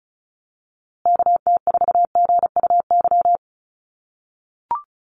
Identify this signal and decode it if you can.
Morse code, heard as a keyed tone, about 34 wpm, 700 Hz. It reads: KT4GUY
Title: USA Call Signs Repeated 2x 35wpm